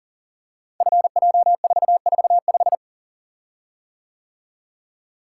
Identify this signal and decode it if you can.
Morse code, heard as a keyed tone, about 38 wpm, 700 Hz. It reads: FJ445
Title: Code Groups - 5 Letter:Numbers - 40wpm